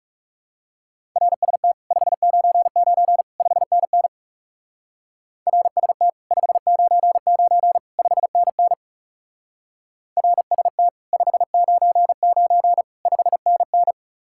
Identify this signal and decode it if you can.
Morse code, heard as a keyed tone, about 38 wpm, 700 Hz. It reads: RST 599 5NN RST 599 5NN RST 599 5NN
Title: Speed-Racing Common QSO Phrases Encore - 30wpm